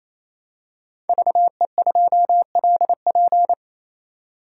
Morse code, heard as a keyed tone, about 28 wpm, 700 Hz. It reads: VE2LP
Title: DXCC-100 Call Signs - 5 Characters - 28wpm